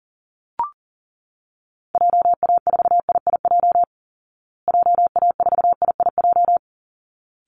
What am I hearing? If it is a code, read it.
Morse code, heard as a keyed tone, about 40 wpm, 700 Hz. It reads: JA4IIJ JA4IIJ
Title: Call Signs Repeated 3x - 40wpm